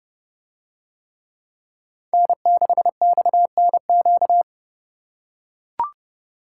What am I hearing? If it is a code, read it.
Morse code, heard as a keyed tone, about 28 wpm, 700 Hz. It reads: N6XNQ